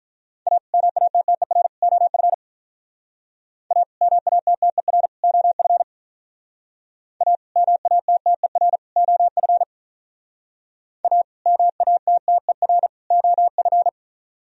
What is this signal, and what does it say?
Morse code, heard as a keyed tone, about 47 wpm, 700 Hz. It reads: A MATTER OF A MATTER OF A MATTER OF A MATTER OF